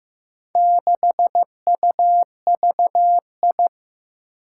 Morse code, heard as a keyed tone, about 15 wpm, 700 Hz. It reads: 6UVI
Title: Four Challenging Letter-Number Combinations 15wpm